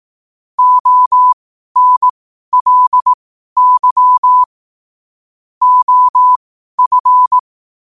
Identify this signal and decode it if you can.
Morse code, heard as a keyed tone, about 18 wpm, spaced as for 12 wpm, 1000 Hz. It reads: ONLY OF
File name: qotdfiles / 12wpm